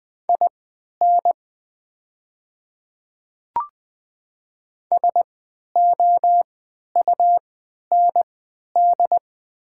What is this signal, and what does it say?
Morse code, heard as a keyed tone, about 20 wpm, spaced as for 11 wpm, 700 Hz. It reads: IN SOUND
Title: 3x Character Spacing - Top 200 Words - 20wpm